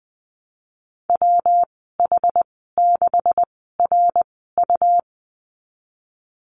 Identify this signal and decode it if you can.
Morse code, heard as a keyed tone, about 20 wpm, 700 Hz. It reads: W H 6 R U
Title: 154 Lesson 25g - Letter p - Call Signs 20wpm